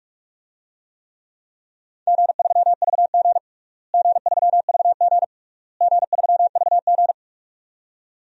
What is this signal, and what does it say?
Morse code, heard as a keyed tone, about 46 wpm, 700 Hz. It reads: G3VG G3VG G3VG
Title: DXCC-100 Call Signs - 4 Characters - Repeated 3x - 45wpm